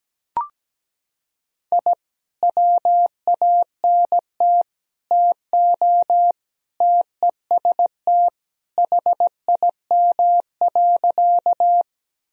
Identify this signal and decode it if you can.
Morse code, heard as a keyed tone, about 17 wpm, 700 Hz. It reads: I WANT TO TEST HIM.
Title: Sentences from Top 1000 Words - Review Part 3 - 17wpm